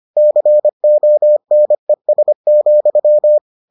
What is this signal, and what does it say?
Morse code, heard as a keyed tone, about 25 wpm, 600 Hz. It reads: CONES,